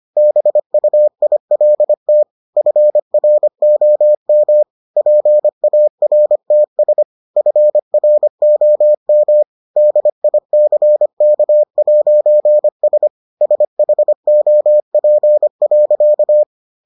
Morse code, heard as a keyed tone, about 25 wpm, 600 Hz. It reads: BUILT FROM PARTS FROM DICK'S SHOP.